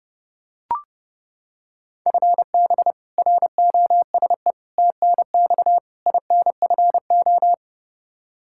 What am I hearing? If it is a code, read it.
Morse code, heard as a keyed tone, about 32 wpm, 700 Hz. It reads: FB ROSE TNX INFO
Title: Practice QSO - Exchange 3 - More Info - 30wpm